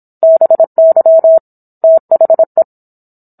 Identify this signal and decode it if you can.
Morse code, heard as a keyed tone, about 27 wpm, 650 Hz. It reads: BY THE